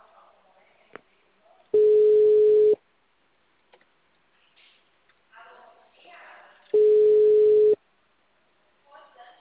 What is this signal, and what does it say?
Ringback tone (Europe, 425 Hz), with speech.